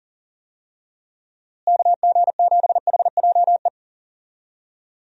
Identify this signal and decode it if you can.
Morse code, heard as a keyed tone, about 40 wpm, 700 Hz. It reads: KG7HJE
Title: USA Call Signs - 6 Characters - Repeated 3x - 40wpm